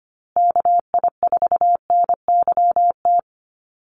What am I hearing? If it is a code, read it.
Morse code, heard as a keyed tone, about 25 wpm, 700 Hz. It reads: KI4NYT
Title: USA Call Signs Repeated 2x 25wpm